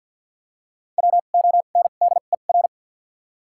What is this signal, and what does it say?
Morse code, heard as a keyed tone, about 52 wpm, spaced as for 36 wpm, 700 Hz. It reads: WONDER
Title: 2x Character Spacing - Top 500 Words - 50wpm